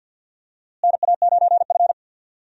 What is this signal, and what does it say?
Morse code, heard as a keyed tone, about 51 wpm, 700 Hz. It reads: NA9F
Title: USA Call Signs - 4 Characters - Repeated 2x - 50wpm